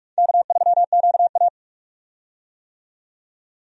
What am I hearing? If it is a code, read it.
Morse code, heard as a keyed tone, about 46 wpm, 700 Hz. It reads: K3QA
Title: USA Call Signs - 4 Characters - 45wpm